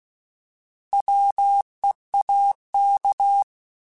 Morse code, heard as a keyed tone, about 16 wpm, 800 Hz. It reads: WEAK
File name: Alphabetical / Alphabetical mp3s / Ws @15